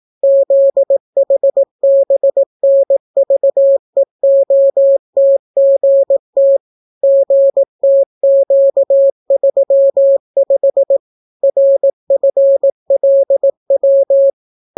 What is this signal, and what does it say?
Morse code, heard as a keyed tone, about 18 wpm, 550 Hz. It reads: ZHBNVEOTGT GTQ35 RFLW